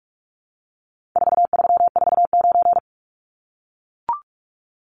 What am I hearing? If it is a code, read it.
Morse code, heard as a keyed tone, about 44 wpm, 700 Hz. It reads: 4349
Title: Numbers - 4-Digit 45wpm